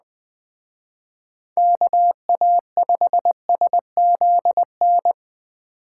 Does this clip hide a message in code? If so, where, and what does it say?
Yes: Morse code from 1.6 to 5.1 s, heard as a keyed tone, about 20 wpm, 700 Hz. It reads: KA5SZN